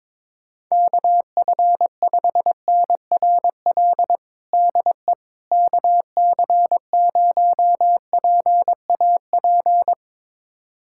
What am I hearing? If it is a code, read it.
Morse code, heard as a keyed tone, about 22 wpm, 700 Hz. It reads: KF5NRL DE KC0PAP
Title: USA Callsign DE Callsign 22wpm